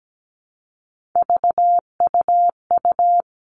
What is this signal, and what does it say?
Morse code, heard as a keyed tone, about 17 wpm, 700 Hz. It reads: VUU
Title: All Cut Numbers - 3-Digit 17wpm